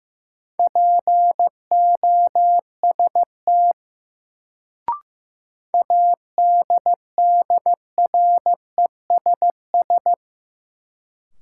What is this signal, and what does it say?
Morse code, heard as a keyed tone, about 15 wpm, 700 Hz. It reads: POST ADDRESS
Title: Sentences from Top 1000 Words - New Word Drill - 15wpm